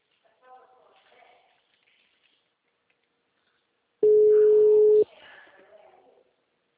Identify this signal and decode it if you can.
Ringback tone (Europe, 425 Hz), with speech.